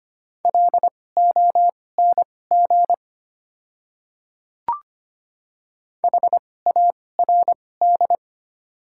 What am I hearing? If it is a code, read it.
Morse code, heard as a keyed tone, about 25 wpm, spaced as for 18 wpm, 690 Hz. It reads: LONG HARD
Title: 2x Character Spacing - Top 300 Words - 25wpm